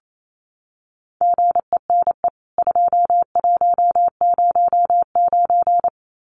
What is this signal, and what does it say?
Morse code, heard as a keyed tone, about 28 wpm, 700 Hz. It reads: GENE 2109
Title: CWT Exchange 28wpm